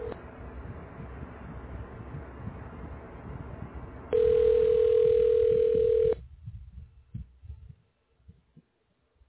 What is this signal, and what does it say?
Ringback tone (North America), 440 + 480 Hz.